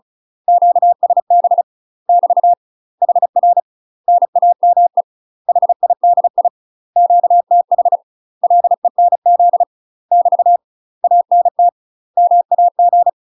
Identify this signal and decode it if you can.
Morse code, heard as a keyed tone, about 36 wpm, 700 Hz. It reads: QSB <BT> HR NAME HIDI QTH LENZ <BT> ANT MAG
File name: UK MASTER QSOs TRAINING / 35WPM / UK MASTER QSOs 0033